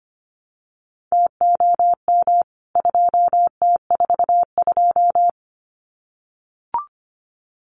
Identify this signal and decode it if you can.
Morse code, heard as a keyed tone, about 25 wpm, 700 Hz. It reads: TOM 2T42
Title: CWT Exchange 25wpm